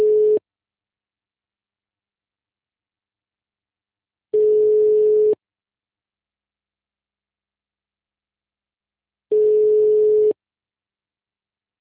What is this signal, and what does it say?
Ringback tone (Europe, 425 Hz).